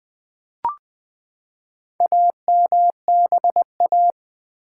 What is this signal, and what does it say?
Morse code, heard as a keyed tone, about 20 wpm, 700 Hz. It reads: AMBA
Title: Build A Word - 4 Letters At A Time - 10-Letter Words - 20wpm